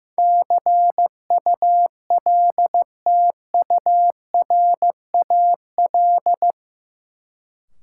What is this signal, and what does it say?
Morse code, heard as a keyed tone, about 15 wpm, 700 Hz. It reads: CULTURAL
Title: Sentences from Top 1500 Words - New Word Drill - 15wpm